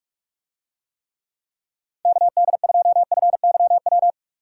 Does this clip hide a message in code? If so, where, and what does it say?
Yes: Morse code from 2.1 to 4.1 s, heard as a keyed tone, about 46 wpm, 700 Hz. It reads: KD2FYW